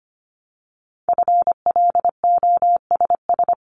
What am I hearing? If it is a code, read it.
Morse code, heard as a keyed tone, about 25 wpm, 700 Hz. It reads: FLOSS
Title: Multi-letter Phonemes 25wpm